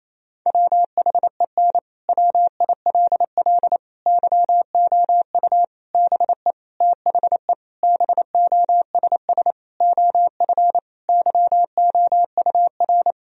Morse code, heard as a keyed tone, about 28 wpm, 700 Hz. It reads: WHEN WILL YOU BE THE BOSS OF YOUR
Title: Sentences from Top 1000 Words - Review Part 4 - 28wpm